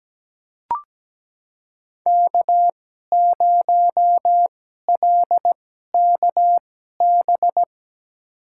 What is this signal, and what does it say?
Morse code, heard as a keyed tone, about 17 wpm, spaced as for 12 wpm, 700 Hz. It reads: K0LKB
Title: USA Call Signs - 2x Character Spacing - 17wpm